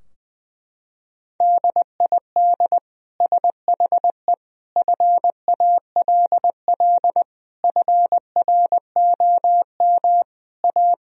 Morse code, heard as a keyed tone, about 21 wpm, 700 Hz. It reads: DID SHE FALL FROM A
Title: Sentences from Top 1000 Words - Review Part 2 - 20wpm